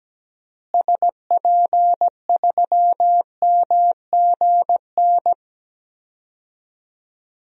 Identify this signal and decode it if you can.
Morse code, heard as a keyed tone, about 17 wpm, 700 Hz. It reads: SP3MGN
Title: DXCC-100 Call Signs - 6 Characters - Repeated 3x - 17wpm